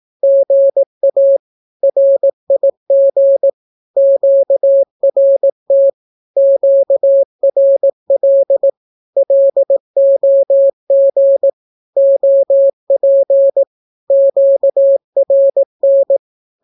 Morse code, heard as a keyed tone, about 18 wpm, 550 Hz. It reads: GA RIG QRT QRL LOG OP QRN
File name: naukapodstaw / skroty / skroty